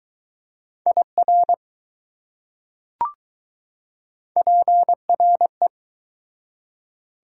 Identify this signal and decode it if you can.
Morse code, heard as a keyed tone, about 20 wpm, 700 Hz. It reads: IR PRE